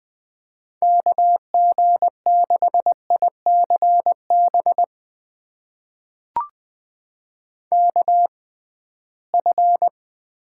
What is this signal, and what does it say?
Morse code, heard as a keyed tone, about 21 wpm, spaced as for 15 wpm, 700 Hz. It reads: KG6ICB KF
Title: USA Call Signs - 6 Characters - 6x Character Spacing - 20wpm